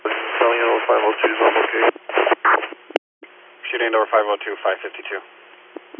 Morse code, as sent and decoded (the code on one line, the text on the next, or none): none